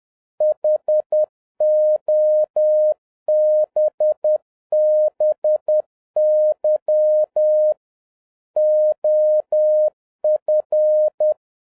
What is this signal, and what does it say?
Morse code, heard as a keyed tone, about 10 wpm, 610 Hz. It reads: HOBBY OF